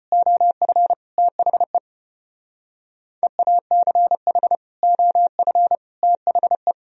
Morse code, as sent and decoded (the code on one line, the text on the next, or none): --- ..-. / - .... / . .- -.-. .... / --- ..-. / - .... .
OF THE EACH OF THE